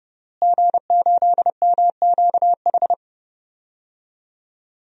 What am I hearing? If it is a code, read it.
Morse code, heard as a keyed tone, about 30 wpm, 700 Hz. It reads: G8MQH
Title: DXCC-100 Call Signs - 5 Characters - Repeated 3x - 30wpm